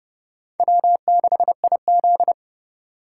Morse code, heard as a keyed tone, about 30 wpm, 700 Hz. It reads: W6IZ
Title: USA Call Signs - 4 Characters - 2x Character Spacing - 30wpm